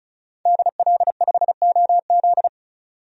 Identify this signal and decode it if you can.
Morse code, heard as a keyed tone, about 36 wpm, 700 Hz. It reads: DL5OZ
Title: DXCC-100 Call Signs - 5 Characters - 35wpm